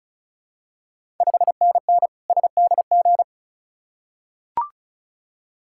Morse code, heard as a keyed tone, about 36 wpm, 700 Hz. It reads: 5NN SDG
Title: Ontario QSO Party Exchange 35wpm